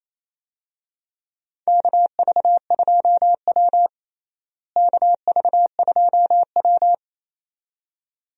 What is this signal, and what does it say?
Morse code, heard as a keyed tone, about 28 wpm, 700 Hz. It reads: KV2W KV2W
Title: USA Call Signs - 4 Characters - Repeated 2x - 28wpm